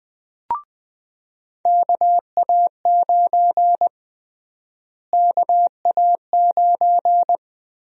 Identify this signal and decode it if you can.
Morse code, heard as a keyed tone, about 21 wpm, 700 Hz. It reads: KA9 KA9
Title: USA Call Sign Prefixes - Repeated 2x - 20wpm